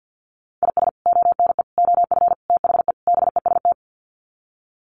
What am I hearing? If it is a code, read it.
Morse code, heard as a keyed tone, about 50 wpm, 700 Hz. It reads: IS ONE OF THE BEST
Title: Sets of 5 Words 50wpm